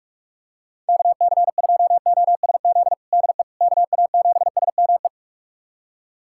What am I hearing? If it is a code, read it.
Morse code, heard as a keyed tone, about 47 wpm, 700 Hz. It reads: KC2OSZ DE KA7SME
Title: USA Callsign DE Callsign - 6 Characters - 45wpm